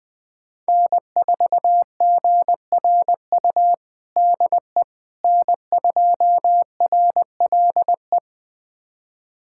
Morse code, heard as a keyed tone, about 20 wpm, 700 Hz. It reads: N4GRU DE N2RLE
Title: USA Callsign DE Callsign - 5 Characters - 20wpm